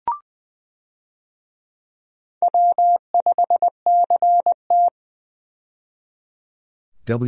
Morse code, heard as a keyed tone, about 20 wpm, 700 Hz. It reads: W5CT
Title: USA Call Signs - 4 Characters - 20wpm